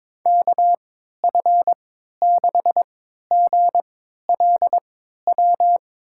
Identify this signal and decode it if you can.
Morse code, heard as a keyed tone, about 22 wpm, spaced as for 13 wpm, 700 Hz. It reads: KF6GLW
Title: USA Call Signs - 6 Characters - 3x Character Spacing - 22wpm